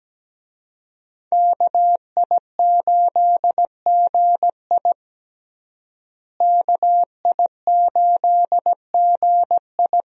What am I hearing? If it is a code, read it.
Morse code, heard as a keyed tone, about 17 wpm, 700 Hz. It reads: KI8GI KI8GI
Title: USA Call Signs - 5 Characters - Repeated 2x - 17wpm